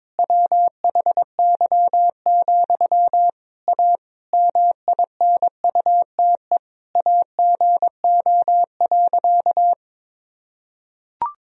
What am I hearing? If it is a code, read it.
Morse code, heard as a keyed tone, about 22 wpm, 700 Hz. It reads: WHY, A MINUTE AGO.
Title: Sentences from Top 500 Words and Abbreviations 22wpm